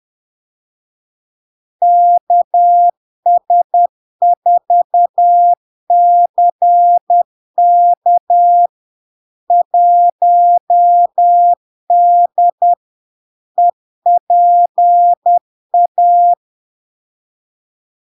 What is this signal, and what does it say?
Morse code, heard as a keyed tone, about 10 wpm, 700 Hz. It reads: KS4CK 1D EPA
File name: LARK-FD-2022-Log-in-CW-10wpm-700-Hz